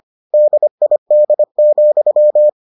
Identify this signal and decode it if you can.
Morse code, heard as a keyed tone, about 25 wpm, 600 Hz. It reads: DID,